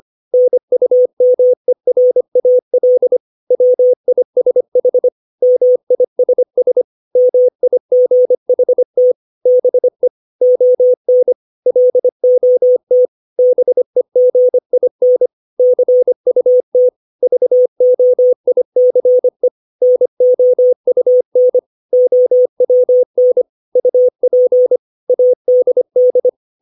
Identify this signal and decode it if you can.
Morse code, heard as a keyed tone, about 25 wpm, 500 Hz. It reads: NUMERAL WISH MISS MIGHT BE ON LOT BEGIN CUT VOICE NOUN OWN UP ADD
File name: 632 words / 632 Words 25WPM